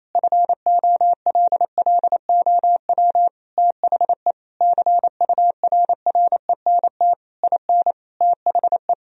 Morse code, heard as a keyed tone, about 28 wpm, 700 Hz. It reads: FOLLOW THE CURRENT IN THE
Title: Sentences from Top 1000 Words - Review Part 4 - 28wpm